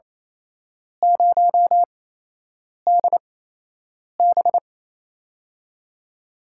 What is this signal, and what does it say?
Morse code, heard as a keyed tone, about 28 wpm, spaced as for 8 wpm, 700 Hz. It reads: 0DB
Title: USA Call Signs - 4 Characters - 8x Character Spacing - 28wpm